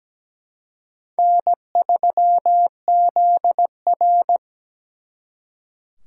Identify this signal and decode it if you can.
Morse code, heard as a keyed tone, about 20 wpm, 700 Hz. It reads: N3ZR
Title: Speed Racing - USA 1x2 Call Signs 17wpm